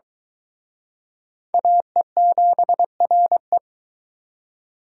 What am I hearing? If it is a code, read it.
Morse code, heard as a keyed tone, about 23 wpm, 700 Hz. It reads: AE7RE